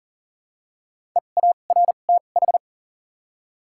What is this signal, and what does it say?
Morse code, heard as a keyed tone, about 41 wpm, spaced as for 29 wpm, 700 Hz. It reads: EARTH
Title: 2x Character Spacing - Top 500 Words - 40wpm